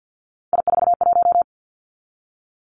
Morse code, heard as a keyed tone, about 50 wpm, 700 Hz. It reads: I41
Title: Three Challenging Letter-Number Combinations 50wpm